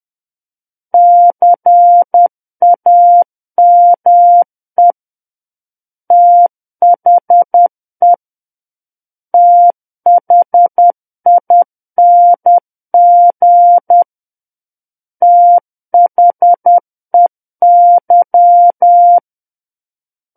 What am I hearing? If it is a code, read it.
Morse code, heard as a keyed tone, about 10 wpm, 700 Hz. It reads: CAME THE THING THEY
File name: Book One / 01-Chapter 1 - 10 WPM